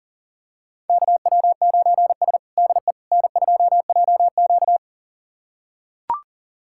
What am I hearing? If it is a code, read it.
Morse code, heard as a keyed tone, about 40 wpm, 700 Hz. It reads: KW9S DE N2JQ
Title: USA Callsign DE Callsign - 4 Characters - 40wpm